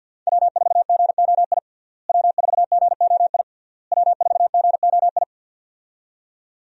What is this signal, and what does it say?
Morse code, heard as a keyed tone, about 52 wpm, 700 Hz. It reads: W4GOI W4GOI W4GOI
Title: USA Call Signs - 5 Characters - Repeated 3x - 50wpm